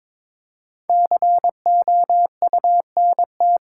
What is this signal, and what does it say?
Morse code, heard as a keyed tone, about 22 wpm, 690 Hz. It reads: COUNT
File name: practice / 22wpm / Easy Sentences with Hints - 22wpm